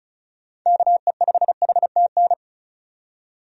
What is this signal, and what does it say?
Morse code, heard as a keyed tone, about 36 wpm, 700 Hz. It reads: KE5HTN
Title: USA Call Signs - 6 Characters - 35wpm